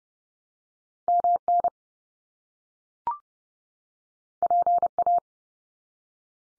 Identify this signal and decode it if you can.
Morse code, heard as a keyed tone, about 30 wpm, 700 Hz. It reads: MN PA
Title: US States 30WPM